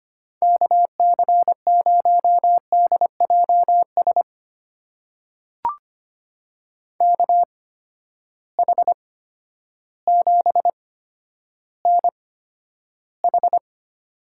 Morse code, heard as a keyed tone, about 25 wpm, spaced as for 11 wpm, 700 Hz. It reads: KC0DJS KH7NH
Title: USA Call Signs - 8x Character Spacing - 25wpm